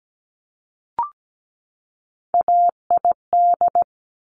Morse code, heard as a keyed tone, about 17 wpm, 700 Hz. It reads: AID
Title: Top Three Letter Combinations in Words 17wpm